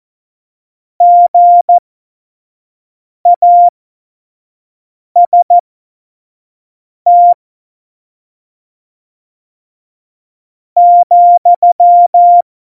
Morse code, heard as a keyed tone, about 14 wpm, spaced as for 4 wpm, 700 Hz. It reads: GAST,
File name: Morsekurs / Witze / Witz 12 Mai 2020, 25 F70